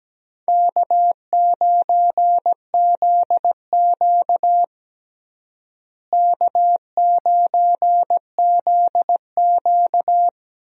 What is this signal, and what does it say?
Morse code, heard as a keyed tone, about 17 wpm, 700 Hz. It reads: K9ZQ K9ZQ
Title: USA Call Signs - 4 Characters - Repeated 2x - 17wpm